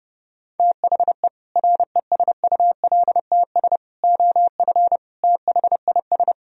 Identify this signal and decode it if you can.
Morse code, heard as a keyed tone, about 30 wpm, 700 Hz. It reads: THE RESULTS OF THIS